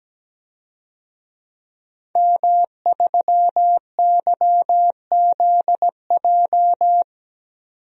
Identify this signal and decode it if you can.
Morse code, heard as a keyed tone, about 17 wpm, 700 Hz. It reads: M3YZJ
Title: DXCC-100 Call Signs - 5 Characters - 17wpm